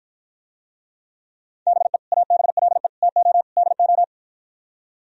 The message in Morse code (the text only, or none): BE ABLE TO DO